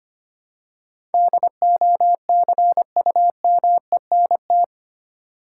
Morse code, heard as a keyed tone, about 25 wpm, 700 Hz. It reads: DOCUMENT
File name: 8 Letter Words - First 3 Letters Given - 25wpm